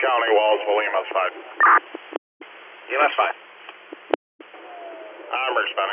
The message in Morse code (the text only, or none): none